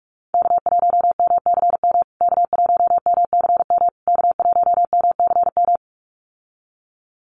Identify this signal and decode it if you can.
Morse code, heard as a keyed tone, about 44 wpm, 700 Hz. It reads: K1MCM K1MCM K1MCM
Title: USA Call Signs Repeated 3x 45wpm